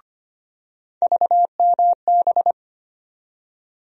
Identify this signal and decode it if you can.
Morse code, heard as a keyed tone, about 25 wpm, 700 Hz. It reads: VMB
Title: USA Call Sign Postfixes - 3 Letters 25wpm